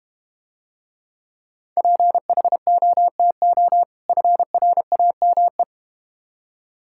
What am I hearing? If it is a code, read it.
Morse code, heard as a keyed tone, about 31 wpm, 700 Hz. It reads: PHOTO FRAME
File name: Difficult Words - Speed-Racing with Hints - 25wpm